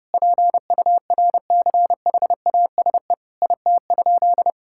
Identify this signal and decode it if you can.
Morse code, heard as a keyed tone, about 30 wpm, 700 Hz. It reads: PURCHASE IT?
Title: Sentences from Top 1000 Words - Review Part 5 - 30wpm